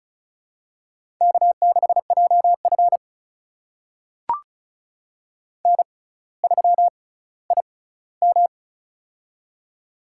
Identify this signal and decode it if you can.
Morse code, heard as a keyed tone, about 36 wpm, spaced as for 18 wpm, 700 Hz. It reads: K6JF N3IM
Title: USA Call Signs - 4 Characters - 6x Character Spacing - 35wpm